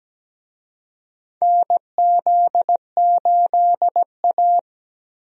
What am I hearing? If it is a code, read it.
Morse code, heard as a keyed tone, about 17 wpm, 700 Hz. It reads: NZ8A
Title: USA Call Signs - 4 Characters - 17wpm